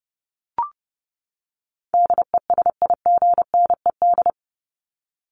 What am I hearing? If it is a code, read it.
Morse code, heard as a keyed tone, about 30 wpm, 700 Hz. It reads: DESIGNED
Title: ED Suffix - Top 100 Words with Base Words 30wpm